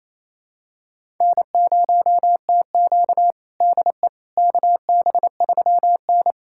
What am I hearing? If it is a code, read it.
Morse code, heard as a keyed tone, about 28 wpm, 700 Hz. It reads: N0TQ DE KB3N
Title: USA Callsign DE Callsign - 4 Characters - 28wpm